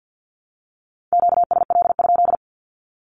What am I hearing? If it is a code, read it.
Morse code, heard as a keyed tone, about 50 wpm, 700 Hz. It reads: QSL?
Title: Q-Codes 50wpm